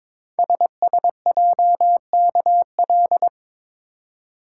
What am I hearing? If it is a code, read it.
Morse code, heard as a keyed tone, about 22 wpm, 690 Hz. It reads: SSJKL
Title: Code Groups - 5 Letters - 22wpm